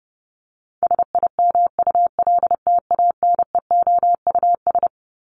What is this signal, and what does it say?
Morse code, heard as a keyed tone, about 30 wpm, 700 Hz. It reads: SIMULTANEOUS
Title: LY Suffix - Top 100 Words with Base Words 30wpm